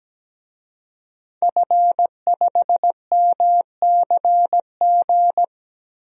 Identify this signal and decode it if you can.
Morse code, heard as a keyed tone, about 17 wpm, 700 Hz. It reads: F5MCG
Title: DXCC-100 Call Signs - 5 Characters - 17wpm